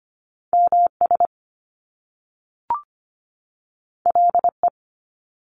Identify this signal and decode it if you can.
Morse code, heard as a keyed tone, about 25 wpm, 700 Hz. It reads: MS LE
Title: Top Two Letter Combinations in Words 25wpm